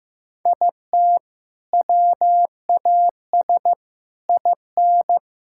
Morse code, heard as a keyed tone, about 16 wpm, 700 Hz. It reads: IT WAS IN
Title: Sentences from Top 1600 Words 15wpm